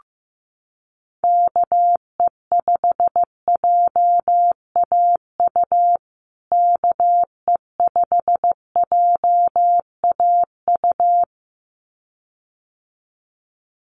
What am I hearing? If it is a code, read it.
Morse code, heard as a keyed tone, about 15 wpm, 700 Hz. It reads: KE5JAU KE5JAU
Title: USA Call Signs Repeated 2x 15wpm